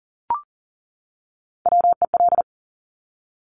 Morse code, heard as a keyed tone, about 40 wpm, 700 Hz. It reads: WEL
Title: Top Three Letter Combinations in Words 40wpm